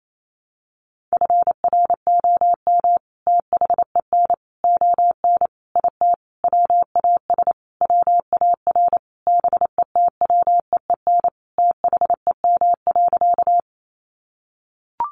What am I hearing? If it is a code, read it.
Morse code, heard as a keyed tone, about 28 wpm, 700 Hz. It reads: FROM THEN ON IT WAS WAR BETWEEN THEM.
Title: Sentences from Top 400 Words 28wpm